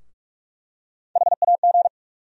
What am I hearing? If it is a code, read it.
Morse code, heard as a keyed tone, about 45 wpm, 700 Hz. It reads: HAG
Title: USA Call Sign Postfixes - 3 Letters 45wpm